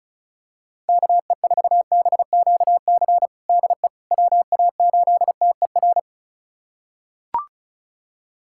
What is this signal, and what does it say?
Morse code, heard as a keyed tone, about 34 wpm, 700 Hz. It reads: KE4BQC DE WA8TER
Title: USA Callsign DE Callsign 35wpm